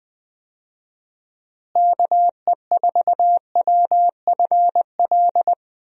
Morse code, heard as a keyed tone, about 20 wpm, 700 Hz. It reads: KE4WFL
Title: USA Call Signs - 6 Characters - Repeated 3x - 20wpm